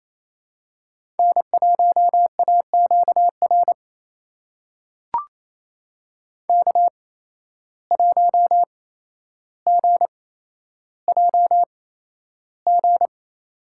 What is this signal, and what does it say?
Morse code, heard as a keyed tone, about 28 wpm, spaced as for 12 wpm, 700 Hz. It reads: N1AQR K1GJG
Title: USA Call Signs - 5 Characters - 8x Character Spacing - 28wpm